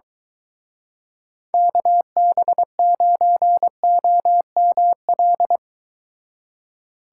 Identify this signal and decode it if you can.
Morse code, heard as a keyed tone, about 23 wpm, 700 Hz. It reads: KB9OML